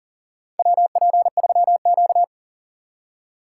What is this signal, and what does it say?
Morse code, heard as a keyed tone, about 40 wpm, 700 Hz. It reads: WP3Q
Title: USA Call Signs - 4 Characters - 2x Character Spacing - 40wpm